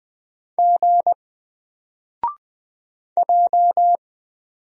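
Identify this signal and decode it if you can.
Morse code, heard as a keyed tone, about 20 wpm, 700 Hz. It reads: G J
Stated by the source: Single letter and number Morse drill at 20 wpm, from N4LCD.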